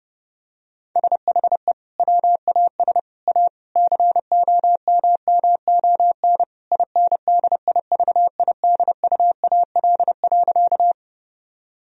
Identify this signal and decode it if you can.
Morse code, heard as a keyed tone, about 32 wpm, 700 Hz. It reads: SHE WAS A COMMON INDIVIDUAL.
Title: Sentences from Top 1000 Words - Review Part 4 - 30wpm